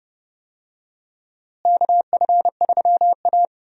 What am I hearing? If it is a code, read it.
Morse code, heard as a keyed tone, about 30 wpm, 700 Hz. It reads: KF3A
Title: USA Call Signs - 4 Characters - Repeated 2x - 30wpm